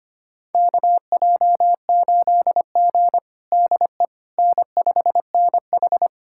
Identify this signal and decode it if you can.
Morse code, heard as a keyed tone, about 25 wpm, 700 Hz. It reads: KJ8G DE N5NH
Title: USA Callsign DE Callsign - 4 Characters - 25wpm